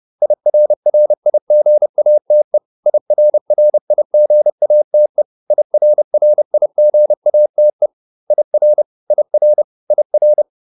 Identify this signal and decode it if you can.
Morse code, heard as a keyed tone, about 30 wpm, 600 Hz. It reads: IRRIGATE IRRIGATE IRRIGATE IR IR IR